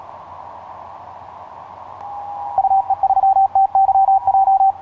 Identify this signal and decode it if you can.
Morse code, heard as a keyed tone, about 37 wpm, 780 Hz. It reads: UE3TYJ
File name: yj0ha ha8be etc 15m cw 10nov2010 0839